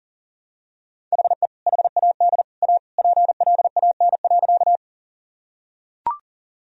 Morse code, heard as a keyed tone, about 40 wpm, 700 Hz. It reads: HE HAD A PLAN.
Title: Sentences from Top 1000 Words - Review Part 2 - 40wpm